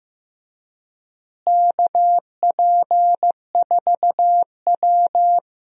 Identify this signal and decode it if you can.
Morse code, heard as a keyed tone, about 15 wpm, 700 Hz. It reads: KP4W
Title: DXCC-100 Call Signs - 4 Characters - Repeated 3x - 15wpm